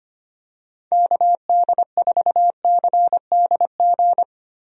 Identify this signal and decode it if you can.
Morse code, heard as a keyed tone, about 25 wpm, 700 Hz. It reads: KD4CDG
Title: USA Call Signs - 6 Characters - Repeated 2x - 25wpm